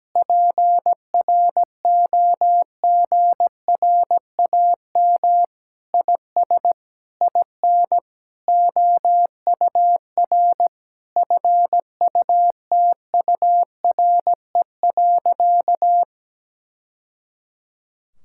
Morse code, heard as a keyed tone, about 17 wpm, 700 Hz. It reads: PROGRAM IS IN OUR FUTURE.
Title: Sentences from Top 1000 Words - Review Part 5 - 17wpm